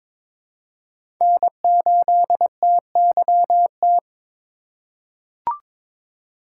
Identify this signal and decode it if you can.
Morse code, heard as a keyed tone, about 22 wpm, 700 Hz. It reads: N8TYT
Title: USA Call Signs - 5 Characters - 3x Character Spacing - 22wpm